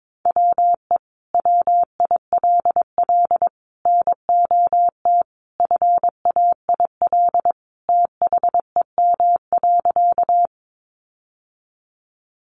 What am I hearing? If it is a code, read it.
Morse code, heard as a keyed tone, about 22 wpm, 700 Hz. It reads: WE WILL NOT FAIL THEM.
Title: Sentences from Top 1100 Words 22wpm